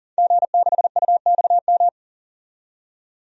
Morse code, heard as a keyed tone, about 40 wpm, 700 Hz. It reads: G6UXM
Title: DXCC-100 Call Signs - 5 Characters - 40wpm